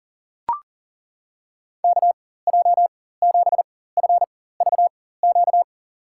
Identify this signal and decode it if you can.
Morse code, heard as a keyed tone, about 40 wpm, spaced as for 19 wpm, 700 Hz. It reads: KJ7FVQ
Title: USA Call Signs - 6 Characters - 4x Character Spacing - 40wpm